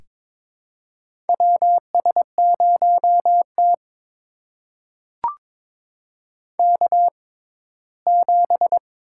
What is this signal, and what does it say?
Morse code, heard as a keyed tone, about 22 wpm, spaced as for 15 wpm, 700 Hz. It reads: WS0T K7
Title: USA Call Signs - 4 Characters - 6x Character Spacing - 22wpm